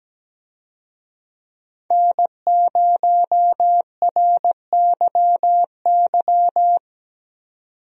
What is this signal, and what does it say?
Morse code, heard as a keyed tone, about 17 wpm, 700 Hz. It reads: N0RYY
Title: USA Call Signs - 5 Characters - 17wpm